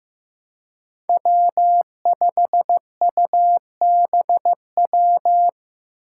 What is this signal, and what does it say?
Morse code, heard as a keyed tone, about 15 wpm, 700 Hz. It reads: W5UBW
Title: Five Challenging Letter-Number Combinations 15wpm